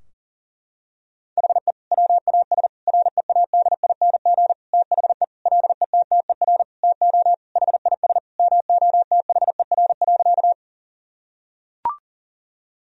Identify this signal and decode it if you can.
Morse code, heard as a keyed tone, about 40 wpm, 700 Hz. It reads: HE WAS READING THE LETTER TO HIS MOTHER.
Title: Sentences from Top 1000 Words - Review Part 3 - 40wpm